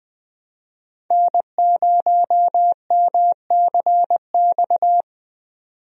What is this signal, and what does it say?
Morse code, heard as a keyed tone, about 21 wpm, 700 Hz. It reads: N0MCX
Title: USA Call Signs - 5 Characters - 20wpm